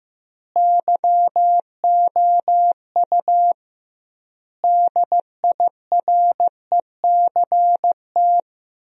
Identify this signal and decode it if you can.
Morse code, heard as a keyed tone, about 15 wpm, 700 Hz. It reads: YOU DIRECT
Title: 2x Word-Spacing - Sentences from Top 400 Words - 15wpm